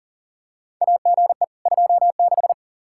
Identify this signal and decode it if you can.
Morse code, heard as a keyed tone, about 40 wpm, 700 Hz. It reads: AGE 26
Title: Practice QSO - Exchange 3 - More Info - 40wpm